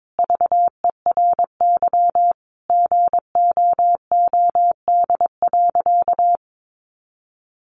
Morse code, heard as a keyed tone, about 22 wpm, 700 Hz. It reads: VERY GOOD.
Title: Sentences from Top 200 Words 22wpm